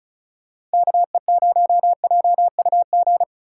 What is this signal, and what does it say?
Morse code, heard as a keyed tone, about 36 wpm, 700 Hz. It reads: KE0JUG
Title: USA Call Signs - 6 Characters - 3x Character Spacing - 35wpm